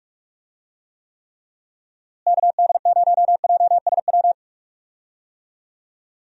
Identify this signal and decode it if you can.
Morse code, heard as a keyed tone, about 46 wpm, 700 Hz. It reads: KD0JSW
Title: USA Call Signs - 6 Characters - Repeated 3x - 45wpm